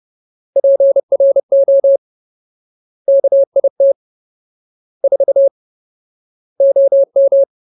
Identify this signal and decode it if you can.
Morse code, heard as a keyed tone, about 30 wpm, 550 Hz. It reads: PRO KIT 4 OM
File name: Ultimate Word List / 30WPM Ultimate